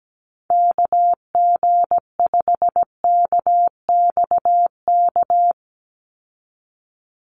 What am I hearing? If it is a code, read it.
Morse code, heard as a keyed tone, about 17 wpm, 700 Hz. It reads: KG5KXK
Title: USA Call Signs 17wpm